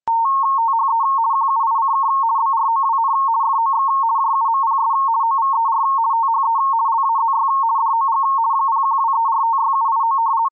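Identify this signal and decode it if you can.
RTTY, about 50 baud. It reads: THE QUICK BROWN FOX JUMPS OVER THE LAZY DOG 0123456789
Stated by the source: RTTY with 170 Hz shift at 50 baud, received on an ESP32-S3 SI4735 pocket SSB receiver.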